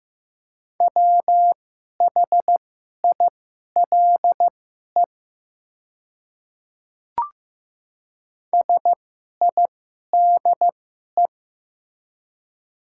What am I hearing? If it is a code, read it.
Morse code, heard as a keyed tone, about 15 wpm, spaced as for 11 wpm, 700 Hz. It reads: WHILE SIDE
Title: 2x Character Spacing - Top 300 Words - 15wpm